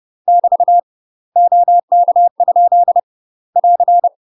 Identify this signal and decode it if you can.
Morse code, heard as a keyed tone, about 30 wpm, 700 Hz. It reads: <BT> OK? <AR>
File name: UK MASTER QSOs TRAINING / 30WPM / UK MASTER QSOs 0017